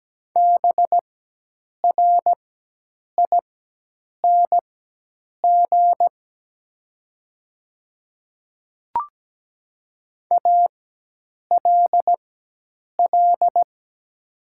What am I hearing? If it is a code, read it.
Morse code, heard as a keyed tone, about 17 wpm, spaced as for 8 wpm, 700 Hz. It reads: BRING ALL
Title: 4x Character Spacing - Top 500 Words - 17wpm